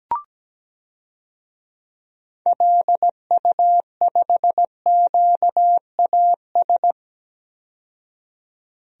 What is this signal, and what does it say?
Morse code, heard as a keyed tone, about 17 wpm, 700 Hz. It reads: LU5QAS
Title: DXCC-100 Call Signs - 6 Characters - 17wpm